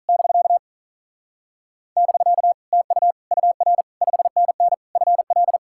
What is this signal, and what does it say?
Morse code, heard as a keyed tone, about 47 wpm, 700 Hz. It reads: <BK> <BK> TU UR 5NN FL